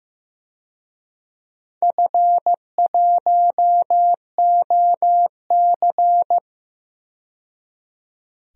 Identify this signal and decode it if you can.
Morse code, heard as a keyed tone, about 15 wpm, 700 Hz. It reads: F1OC
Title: DXCC-100 Call Signs - 4 Characters - 15wpm